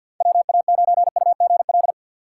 Morse code, heard as a keyed tone, about 53 wpm, 700 Hz. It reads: WA9UGL
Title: USA Call Signs - 6x Character Spacing - 50wpm